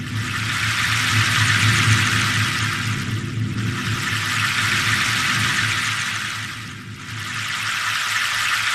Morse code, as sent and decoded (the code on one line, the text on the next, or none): none